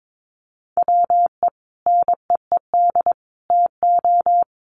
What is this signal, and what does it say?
Morse code, heard as a keyed tone, about 22 wpm, 700 Hz. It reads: WE NEED TO
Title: Sentences from Top 1400 Words 22wpm